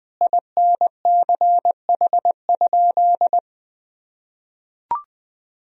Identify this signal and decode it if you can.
Morse code, heard as a keyed tone, about 20 wpm, 700 Hz. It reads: INCH?
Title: Sentences from Top 1000 Words - Review Part 3 - 20wpm